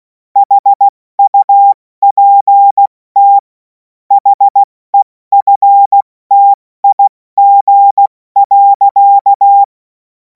Morse code, heard as a keyed tone, about 16 wpm, 800 Hz. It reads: HUPT HEFTIG.
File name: Morsekurs / Witze / Witz 05. Mai 2026 70 F80